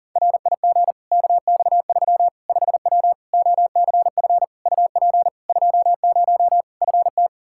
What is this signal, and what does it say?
Morse code, heard as a keyed tone, about 40 wpm, 700 Hz. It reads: RIG KX3 5W OCF UP 20 FT
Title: Practice QSO - Exchange 2 - Info - 40wpm